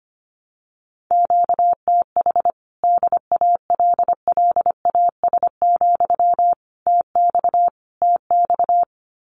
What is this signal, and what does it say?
Morse code, heard as a keyed tone, about 25 wpm, 700 Hz. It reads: QTH DALLAS, TX TX
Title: Common QSO Phrases Encore 25wpm